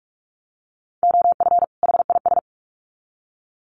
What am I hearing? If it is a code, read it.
Morse code, heard as a keyed tone, about 46 wpm, 700 Hz. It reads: OF HIS
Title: Sets of 2 Words 45wpm